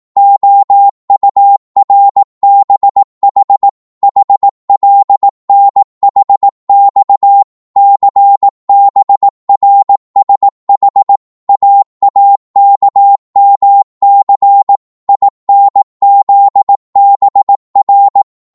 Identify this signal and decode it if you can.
Morse code, heard as a keyed tone, about 18 wpm, 800 Hz. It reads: OURBH HLNHX CBRSH AAKMC INZBR